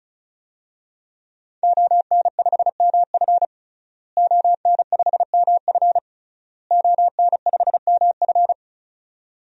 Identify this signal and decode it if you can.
Morse code, heard as a keyed tone, about 36 wpm, 700 Hz. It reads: ON5MF ON5MF ON5MF
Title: DXCC-100 Call Signs - 5 Characters - Repeated 3x - 35wpm